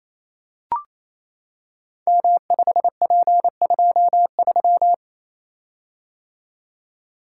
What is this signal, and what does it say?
Morse code, heard as a keyed tone, about 27 wpm, 700 Hz. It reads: M5P23
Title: Code Groups - 5 Letter:Numbers - 28wpm